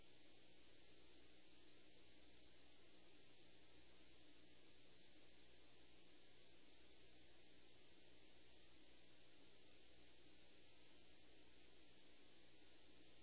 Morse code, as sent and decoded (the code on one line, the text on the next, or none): none